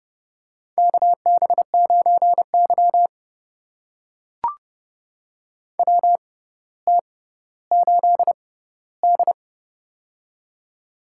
Morse code, heard as a keyed tone, about 30 wpm, spaced as for 15 wpm, 700 Hz. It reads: KB9Y WT8D
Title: USA Call Signs - 4 Characters - 6x Character Spacing - 30wpm